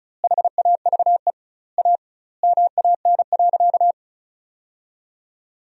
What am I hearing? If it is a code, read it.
Morse code, heard as a keyed tone, about 36 wpm, 700 Hz. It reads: HAVE A MAN.
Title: 2x Word-Spacing - Sentences from Top 300 Words - 35wpm